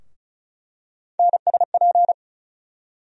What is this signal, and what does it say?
Morse code, heard as a keyed tone, about 36 wpm, 700 Hz. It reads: NSP
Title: USA Call Sign Postfixes - 3 Letters 35wpm